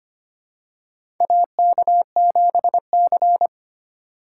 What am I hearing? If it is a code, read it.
Morse code, heard as a keyed tone, about 25 wpm, 700 Hz. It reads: AK7C
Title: USA Call Signs - 4 Characters - 2x Character Spacing - 25wpm